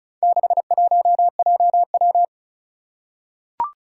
Morse code, heard as a keyed tone, about 36 wpm, 700 Hz. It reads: 61JW
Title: Four Challenging Letter-Number Combinations 35wpm